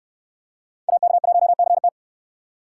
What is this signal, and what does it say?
Morse code, heard as a keyed tone, about 66 wpm, 700 Hz. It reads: AK1CT